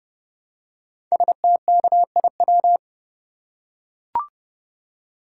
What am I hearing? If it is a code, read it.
Morse code, heard as a keyed tone, about 30 wpm, 700 Hz. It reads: STKIW
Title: Code Groups - 5 Letters - 30wpm